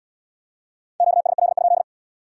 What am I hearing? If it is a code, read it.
Morse code, heard as a keyed tone, about 65 wpm, 700 Hz. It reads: QSL?